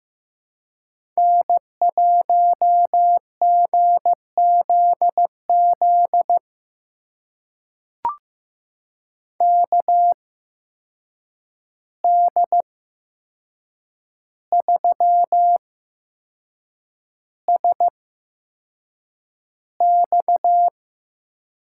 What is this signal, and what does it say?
Morse code, heard as a keyed tone, about 15 wpm, spaced as for 6 wpm, 700 Hz. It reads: N1GZZ KD3SX
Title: USA Call Signs - 5 Characters - 8x Character Spacing - 15wpm